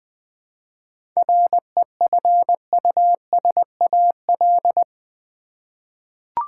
Morse code, heard as a keyed tone, about 20 wpm, 700 Hz. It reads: REFUSAL
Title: 7 Letter Words - First 3 Letters Given - 20wpm